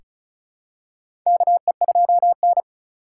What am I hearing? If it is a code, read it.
Morse code, heard as a keyed tone, about 36 wpm, 700 Hz. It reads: KE2N
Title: USA Call Signs - 4 Characters - 4x Character Spacing - 35wpm